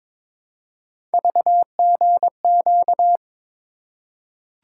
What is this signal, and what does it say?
Morse code, heard as a keyed tone, about 22 wpm, 700 Hz. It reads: VGQ
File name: practice / 22wpm / Three Letter-Number Combinations - 22wpm